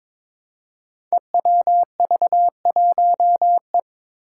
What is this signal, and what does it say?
Morse code, heard as a keyed tone, about 22 wpm, 700 Hz. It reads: EWV1E
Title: Five Challenging Letter-Number Combinations 22wpm